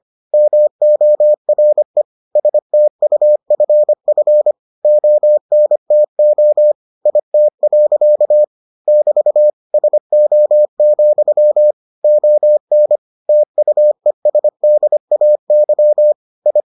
Morse code, heard as a keyed tone, about 25 wpm, 600 Hz. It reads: MORE STUFF ONTO IT. <BT> SO, ON TUESDAY I